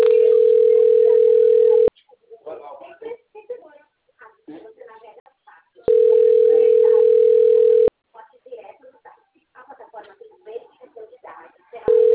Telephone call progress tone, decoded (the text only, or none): ringback tone